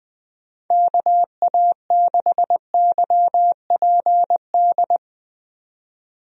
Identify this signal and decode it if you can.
Morse code, heard as a keyed tone, about 20 wpm, 700 Hz. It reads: KA6YPD
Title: USA Call Signs - 6 Characters - 20wpm